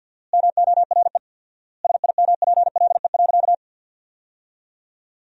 Morse code, heard as a keyed tone, about 53 wpm, 700 Hz. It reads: MORE SIMPLE.
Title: 4x Word-Spacing - Sentences from Top 500 Words - 50wpm